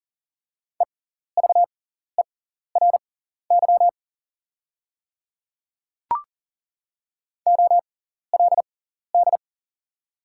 Morse code, heard as a keyed tone, about 40 wpm, spaced as for 14 wpm, 690 Hz. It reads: EVERY OLD
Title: 6x Character Spacing - Top 200 Words - 40wpm